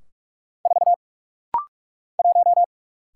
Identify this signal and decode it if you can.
Morse code, heard as a keyed tone, about 44 wpm, 700 Hz. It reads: 4 1
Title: Single Letter-Number - Rapid-Fire 45wpm